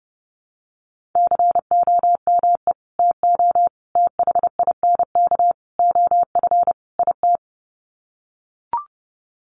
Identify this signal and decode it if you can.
Morse code, heard as a keyed tone, about 30 wpm, 700 Hz. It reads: COME TO THINK OF IT
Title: Sets of 5 Words Encore 30wpm